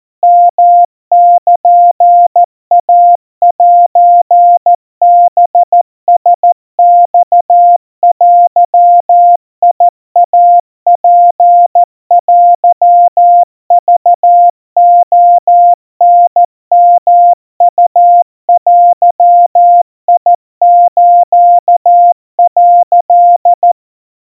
Morse code (Japanese Wabun code), heard as a keyed tone, about 14 wpm, 700 Hz. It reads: ヨルイセハラマデイツテクレタヨウデス」
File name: kawa / Morse sample / kamyu 50 cpm 700 Hz Crispy